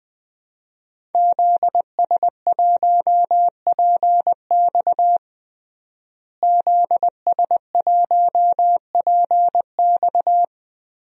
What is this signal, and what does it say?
Morse code, heard as a keyed tone, about 20 wpm, 700 Hz. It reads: ZS1PX ZS1PX
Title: DXCC-100 Call Signs - 5 Characters - Repeated 2x - 20wpm